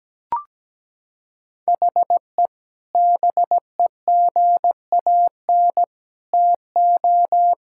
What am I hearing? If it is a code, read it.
Morse code, heard as a keyed tone, about 17 wpm, 700 Hz. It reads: HE BEGAN TO
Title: Sentences from Top 1600 Words 17wpm